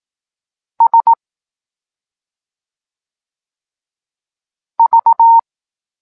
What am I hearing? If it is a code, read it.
Morse code, heard as a keyed tone, about 18 wpm, 920 Hz. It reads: S V
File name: Tekst-9-rownomierny